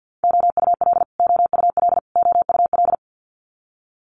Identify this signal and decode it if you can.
Morse code, heard as a keyed tone, about 50 wpm, 690 Hz. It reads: OUL OUL OUL
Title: Multi-letter Phonemes Repeated 3x 50wpm